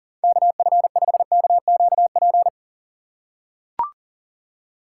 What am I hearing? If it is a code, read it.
Morse code, heard as a keyed tone, about 40 wpm, 700 Hz. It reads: KF5KQP
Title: USA Call Signs - 6 Characters - 6x Character Spacing - 40wpm